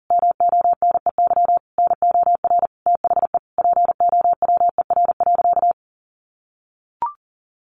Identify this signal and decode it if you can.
Morse code, heard as a keyed tone, about 40 wpm, 700 Hz. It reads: MONEY NOR THE POWER.
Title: Sentences from Top 1500 Words 40wpm